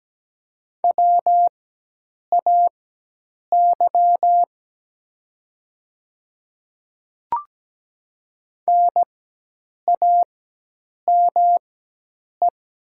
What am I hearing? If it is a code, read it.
Morse code, heard as a keyed tone, about 17 wpm, spaced as for 8 wpm, 700 Hz. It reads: WAY NAME
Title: 4x Character Spacing - Top 200 Words - 17wpm